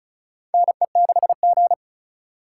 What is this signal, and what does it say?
Morse code, heard as a keyed tone, about 36 wpm, 690 Hz. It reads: NE6G
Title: USA Call Signs - 4 Characters - 2x Character Spacing - 35wpm